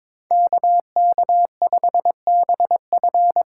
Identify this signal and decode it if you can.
Morse code, heard as a keyed tone, about 22 wpm, 700 Hz. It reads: KK5BF
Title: USA Call Signs - 5 Characters - 3x Character Spacing - 22wpm